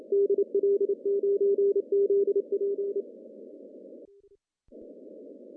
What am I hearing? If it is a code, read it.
Morse code, heard as a keyed tone, about 27 wpm, 410 Hz. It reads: DL9ZP
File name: CW / cqww-cw-k5tr